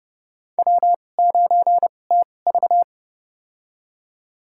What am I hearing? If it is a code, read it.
Morse code, heard as a keyed tone, about 30 wpm, spaced as for 22 wpm, 700 Hz. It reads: W9TV
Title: USA Call Signs - 4 Characters - 2x Character Spacing - 30wpm